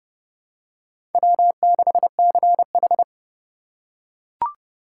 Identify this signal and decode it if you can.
Morse code, heard as a keyed tone, about 30 wpm, 700 Hz. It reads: W6CH
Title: USA Call Signs - 4 Characters - 2x Character Spacing - 30wpm